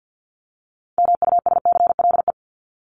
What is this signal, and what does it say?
Morse code, heard as a keyed tone, about 50 wpm, 700 Hz. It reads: MUSCLE
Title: Top 1251-1500 Words 50wpm